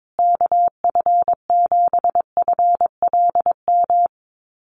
Morse code, heard as a keyed tone, about 22 wpm, 700 Hz. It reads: KF7FLM
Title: USA Call Signs Repeated 3x 22wpm